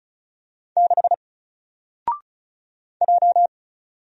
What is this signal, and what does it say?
Morse code, heard as a keyed tone, about 36 wpm, 700 Hz. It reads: 6 J
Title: Single Letter-Number 35wpm